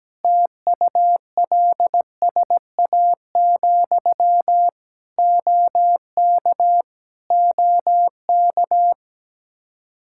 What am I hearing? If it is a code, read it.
Morse code, heard as a keyed tone, about 17 wpm, 700 Hz. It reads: TULSA, OK OK
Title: Practice QSO - Exchange 1 - Beginning - 17wpm